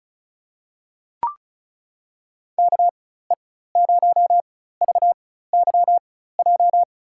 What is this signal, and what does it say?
Morse code, heard as a keyed tone, about 36 wpm, spaced as for 16 wpm, 700 Hz. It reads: KE0VYJ
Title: USA Call Signs - 4x Character Spacing - 35wpm